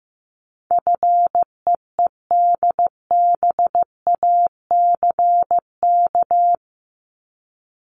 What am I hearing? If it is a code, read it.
Morse code, heard as a keyed tone, about 15 wpm, 700 Hz. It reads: FEEDBACK
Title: Top 1501-1750 Words 15wpm